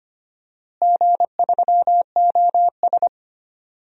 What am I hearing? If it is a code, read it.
Morse code, heard as a keyed tone, about 25 wpm, 700 Hz. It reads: G3OS
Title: DXCC-100 Call Signs - 4 Characters - Repeated 2x - 25wpm